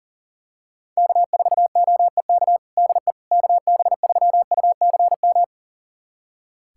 Morse code, heard as a keyed tone, about 40 wpm, 700 Hz. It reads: K4OEK DE KB3UCM
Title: USA Callsign DE Callsign 40wpm